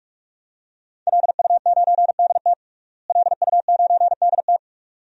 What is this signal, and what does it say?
Morse code, heard as a keyed tone, about 46 wpm, 700 Hz. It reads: LU9DT LU9DT
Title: DXCC-100 Call Signs - 5 Characters - Repeated 2x - 45wpm